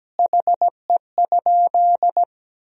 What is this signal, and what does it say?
Morse code, heard as a keyed tone, about 17 wpm, 700 Hz. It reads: HE?
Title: 4x Word-Spacing - Sentences from Top 200 Words - 17wpm